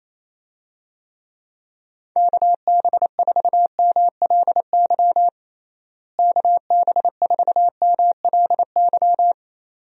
Morse code, heard as a keyed tone, about 28 wpm, 700 Hz. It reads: KB4MLY KB4MLY
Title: USA Call Signs - 6 Characters - Repeated 2x - 28wpm